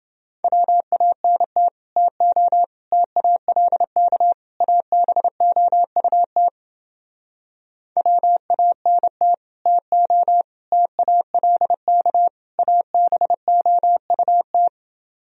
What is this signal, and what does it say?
Morse code, heard as a keyed tone, about 32 wpm, 700 Hz. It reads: WANT TO TALK ABOUT WANT TO TALK ABOUT